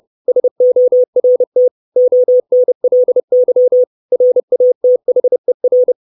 Morse code, heard as a keyed tone, about 30 wpm, 500 Hz. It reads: SORT ONLY RATHER
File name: 632 Words 30WPM